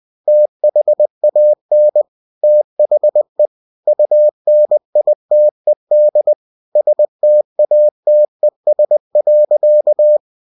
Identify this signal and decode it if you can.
Morse code, heard as a keyed tone, about 20 wpm, 600 Hz. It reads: THAN THE UNITED STATES.
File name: cw-books / winnie-the-pooh / winnie-20wpm-page-0114